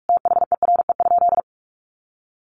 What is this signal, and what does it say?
Morse code, heard as a keyed tone, about 44 wpm, 700 Hz. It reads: THERE?
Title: Sentences from Top 600 Words 45wpm